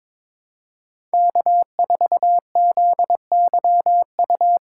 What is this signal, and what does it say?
Morse code, heard as a keyed tone, about 22 wpm, 700 Hz. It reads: K4ZYU
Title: USA Call Signs - 5 Characters - 3x Character Spacing - 22wpm